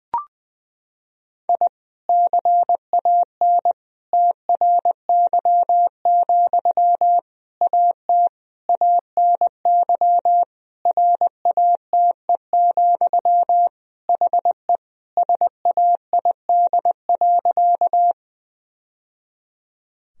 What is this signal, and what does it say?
Morse code, heard as a keyed tone, about 20 wpm, 700 Hz. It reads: I CAN TRY, AT ANY RATE, HE SAID.
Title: Sentences from Top 1000 Words - Review Part 4 - 20wpm